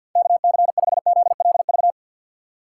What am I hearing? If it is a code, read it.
Morse code, heard as a keyed tone, about 50 wpm, 700 Hz. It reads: KC5ZLV
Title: USA Call Signs - 6 Characters - 3x Character Spacing - 50wpm